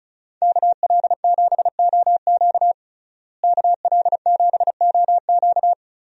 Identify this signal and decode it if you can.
Morse code, heard as a keyed tone, about 36 wpm, 700 Hz. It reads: KL7OQ KL7OQ
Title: DXCC-100 Call Signs - 5 Characters - Repeated 3x - 35wpm